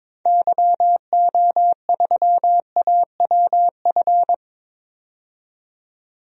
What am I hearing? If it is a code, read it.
Morse code, heard as a keyed tone, about 21 wpm, 700 Hz. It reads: YO3AWF
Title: DXCC-100 Call Signs - 6 Characters - 22wpm